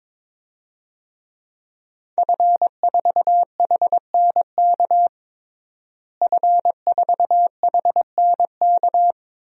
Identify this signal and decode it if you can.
Morse code, heard as a keyed tone, about 22 wpm, 700 Hz. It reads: F4HNK F4HNK
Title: DXCC-100 Call Signs - 5 Characters - Repeated 2x - 22wpm